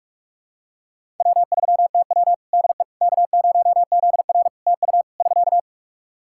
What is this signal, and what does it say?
Morse code, heard as a keyed tone, about 46 wpm, 700 Hz. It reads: W3TW DE K0ZR TU <SK>
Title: K1USN SST QSO Practice 45wpm